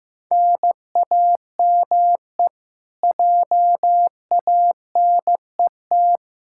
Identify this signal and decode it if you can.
Morse code, heard as a keyed tone, about 15 wpm, 700 Hz. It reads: NAME JANET
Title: K1USN SST QSO Practice 15wpm